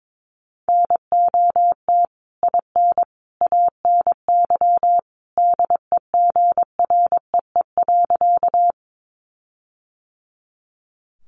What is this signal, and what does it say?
Morse code, heard as a keyed tone, about 22 wpm, 700 Hz. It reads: NOT IN ANY DEGREE.
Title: Sentences from Top 1500 Words 22wpm